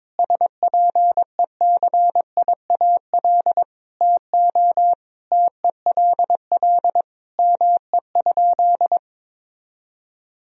Morse code, heard as a keyed tone, about 22 wpm, 700 Hz. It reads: SPECIAL TO TELL ME?
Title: Sentences from Top 1000 Words - Review Part 3 - 22wpm